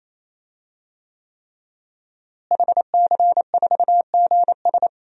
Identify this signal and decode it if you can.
Morse code, heard as a keyed tone, about 28 wpm, 700 Hz. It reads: HC4GS
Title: DXCC-100 Call Signs - 5 Characters - Repeated 3x - 28wpm